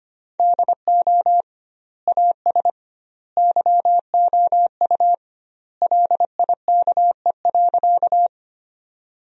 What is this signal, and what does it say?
Morse code, heard as a keyed tone, about 25 wpm, 700 Hz. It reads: DO AS YOU LIKE.
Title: 2x Word-Spacing - Sentences from Top 100 Words - 25wpm